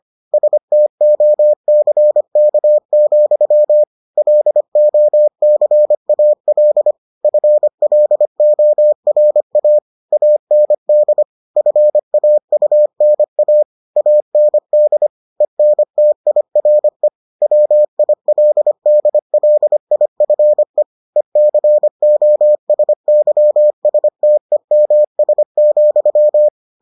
Morse code, heard as a keyed tone, about 25 wpm, 600 Hz. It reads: STOCK, LOCAL FLORA AND FAUNA AND ENTIRE WILDLIFE ECOSYSTEMS,